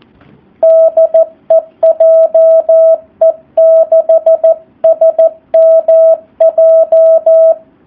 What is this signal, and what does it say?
Morse code, heard as a keyed tone, about 14 wpm, 650 Hz, with radio noise. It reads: DE JE6SMJ
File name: jh8cbh / 2019cw / SP04- w